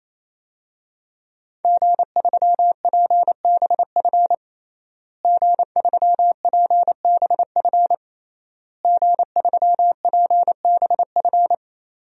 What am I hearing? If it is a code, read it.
Morse code, heard as a keyed tone, about 28 wpm, 700 Hz. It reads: G3PBF G3PBF G3PBF
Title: DXCC-100 Call Signs - 5 Characters - Repeated 3x - 28wpm